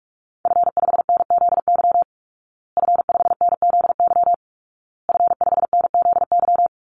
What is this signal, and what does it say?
Morse code, heard as a keyed tone, about 44 wpm, 700 Hz. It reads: F5NZY F5NZY F5NZY
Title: Call Signs Repeated 2x - 45wpm